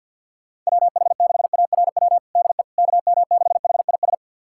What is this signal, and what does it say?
Morse code, heard as a keyed tone, about 51 wpm, 700 Hz. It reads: WH6ARW DE KM6HIS
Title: USA Callsign DE Callsign - 6 Characters - 50wpm